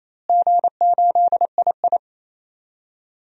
Morse code, heard as a keyed tone, about 28 wpm, 700 Hz. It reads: G8II
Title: DXCC-100 Call Signs - 4 Characters - Repeated 2x - 28wpm